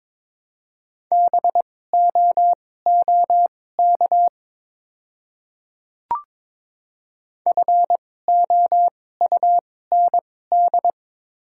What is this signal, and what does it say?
Morse code, heard as a keyed tone, about 22 wpm, spaced as for 16 wpm, 690 Hz. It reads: BOOK FOUND
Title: 2x Character Spacing - Top 500 Words - 22wpm